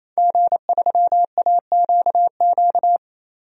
Morse code, heard as a keyed tone, about 28 wpm, 700 Hz. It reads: G3AQQ
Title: DXCC-100 Call Signs - 5 Characters - 28wpm